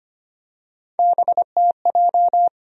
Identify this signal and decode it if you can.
Morse code, heard as a keyed tone, about 25 wpm, 700 Hz. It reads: BTJ
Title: USA Call Sign Postfixes - 3 Letters 25wpm